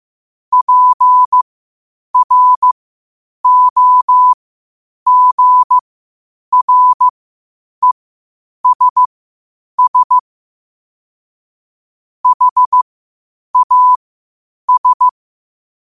Morse code, heard as a keyed tone, about 15 wpm, spaced as for 8 wpm, 1000 Hz. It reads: PROGRESS HAS
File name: qotdfiles / 8wpm